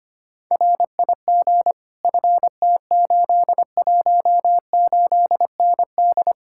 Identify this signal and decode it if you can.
Morse code, heard as a keyed tone, about 24 wpm, 700 Hz. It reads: RIG FT818ND
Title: Speed-Racing Common QSO Phrases Encore - 25wpm